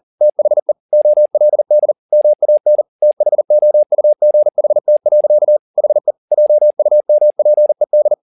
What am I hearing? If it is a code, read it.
Morse code, heard as a keyed tone, about 40 wpm, 600 Hz. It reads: THE OLD MAN THOUGHT. HE JUMPED